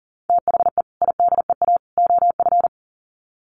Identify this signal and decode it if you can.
Morse code, heard as a keyed tone, about 40 wpm, 700 Hz. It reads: THE IDEA OF
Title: Sets of 3 Words 40wpm